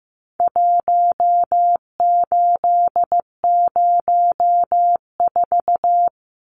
Morse code, heard as a keyed tone, about 15 wpm, 700 Hz. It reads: 1804
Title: Numbers - Contest Style - 4-Digit 15wpm